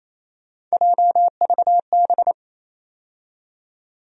Morse code, heard as a keyed tone, about 28 wpm, 700 Hz. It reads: JVB
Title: USA Call Sign Postfixes - 3 Letters 28wpm